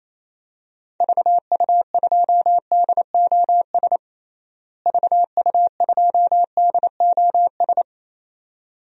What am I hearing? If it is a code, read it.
Morse code, heard as a keyed tone, about 28 wpm, 700 Hz. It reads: VU2DOS VU2DOS
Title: DXCC-100 Call Signs - 6 Characters - Repeated 2x - 28wpm